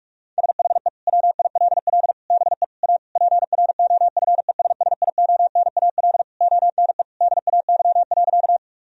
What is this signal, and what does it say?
Morse code, heard as a keyed tone, about 46 wpm, 700 Hz. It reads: SHE WILL BE A PROFESSIONAL ONE DAY.
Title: Sentences from Top 1000 Words - Review Part 4 - 45wpm